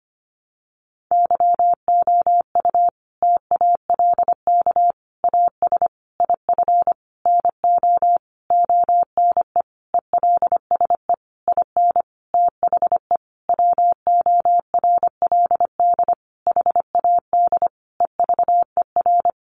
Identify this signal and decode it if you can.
Morse code, heard as a keyed tone, about 25 wpm, 700 Hz. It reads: YOU TALK AS IF NO ONE ELSE IN THE WORLD HAD EVER
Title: Sentences from Top 1100 Words 25wpm